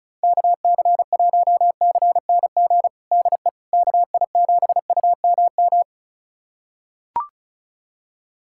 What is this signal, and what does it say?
Morse code, heard as a keyed tone, about 34 wpm, 700 Hz. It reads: KC1CNG DE KI7UMM
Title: USA Callsign DE Callsign 35wpm